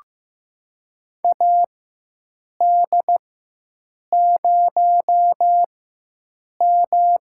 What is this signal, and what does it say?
Morse code, heard as a keyed tone, about 16 wpm, spaced as for 7 wpm, 700 Hz. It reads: AD0M
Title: USA Call Signs - 4 Characters - 4x Character Spacing - 15wpm